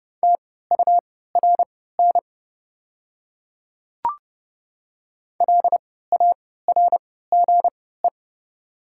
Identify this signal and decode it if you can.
Morse code, heard as a keyed tone, about 30 wpm, spaced as for 17 wpm, 700 Hz. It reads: TURN LARGE
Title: 3x Character Spacing - Top 200 Words - 30wpm